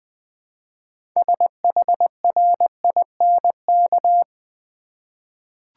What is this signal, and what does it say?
Morse code, heard as a keyed tone, about 21 wpm, 700 Hz. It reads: SHRINK
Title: 6 Letter Words - First 3 Letters Given - 20wpm